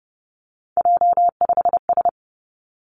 Morse code, heard as a keyed tone, about 32 wpm, 700 Hz. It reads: J5S
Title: Three Challenging Letter-Number Combinations 30wpm